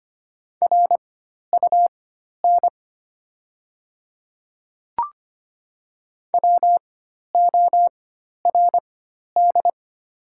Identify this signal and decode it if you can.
Morse code, heard as a keyed tone, about 25 wpm, spaced as for 12 wpm, 690 Hz. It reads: RUN WORD
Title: 4x Character Spacing - Top 500 Words - 25wpm